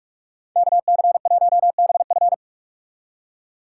Morse code, heard as a keyed tone, about 59 wpm, 700 Hz. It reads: KC1BF